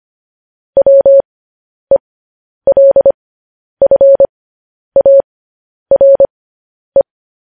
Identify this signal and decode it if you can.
Morse code, heard as a keyed tone, about 25 wpm, spaced as for 10 wpm, 550 Hz. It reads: WELFARE
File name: podcast / News headlines-1x-25x10